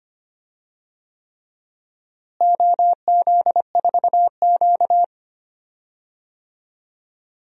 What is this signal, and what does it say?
Morse code, heard as a keyed tone, about 25 wpm, 700 Hz. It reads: OZ4Q
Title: DXCC-100 Call Signs - 4 Characters - Repeated 3x - 25wpm